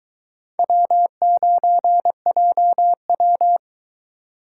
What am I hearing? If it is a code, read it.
Morse code, heard as a keyed tone, about 23 wpm, 700 Hz. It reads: W9JW